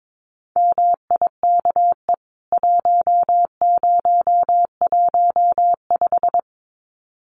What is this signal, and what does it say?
Morse code, heard as a keyed tone, about 22 wpm, 700 Hz. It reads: MIKE 1015
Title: CWT Exchange 22wpm